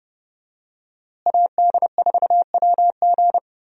Morse code, heard as a keyed tone, about 30 wpm, 700 Hz. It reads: AD4WG
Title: USA Call Signs - 5 Characters - Repeated 2x - 30wpm